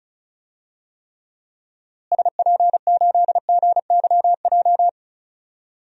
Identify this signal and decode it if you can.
Morse code, heard as a keyed tone, about 34 wpm, 700 Hz. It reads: SP8GYJ
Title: DXCC-100 Call Signs - 6 Characters - Repeated 2x - 35wpm